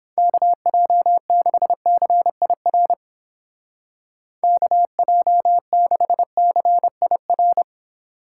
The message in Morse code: KJ6CIR KJ6CIR

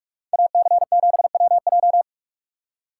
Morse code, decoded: AC7WJ

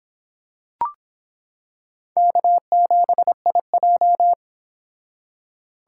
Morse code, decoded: K7IJ

